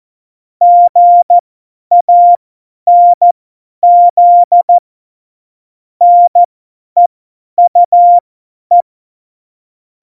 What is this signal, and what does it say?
Morse code, heard as a keyed tone, about 14 wpm, spaced as for 10 wpm, 700 Hz. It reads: GANZ NEUE
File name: Morsekurs / Witze / Witz 13 March 2018, 50 F70